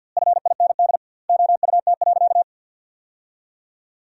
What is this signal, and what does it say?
Morse code, heard as a keyed tone, about 47 wpm, 700 Hz. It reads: FIND OUT.
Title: 2x Word-Spacing - Sentences from Top 200 Words - 50wpm